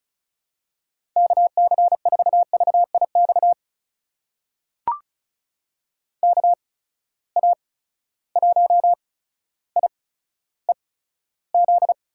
Morse code, heard as a keyed tone, about 36 wpm, spaced as for 15 wpm, 700 Hz. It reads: KC4VIX KA1IEZ
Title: USA Call Signs - 6 Characters - 8x Character Spacing - 35wpm